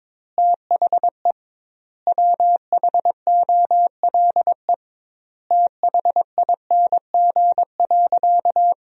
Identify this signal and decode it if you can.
Morse code, heard as a keyed tone, about 22 wpm, 700 Hz. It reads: THE WHOLE THING.
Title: 2x Word-Spacing - Sentences from Top 400 Words - 22wpm